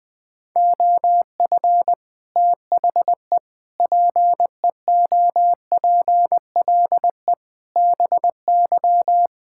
Morse code, heard as a keyed tone, about 20 wpm, 700 Hz. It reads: OF THE PEOPLE BY
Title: Sets of 9 Words - 20wpm